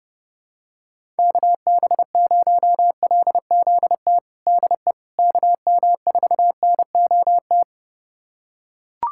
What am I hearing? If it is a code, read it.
Morse code, heard as a keyed tone, about 32 wpm, 700 Hz. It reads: KB0LZT DE KM4NOT
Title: USA Callsign DE Callsign - 6 Characters - 30wpm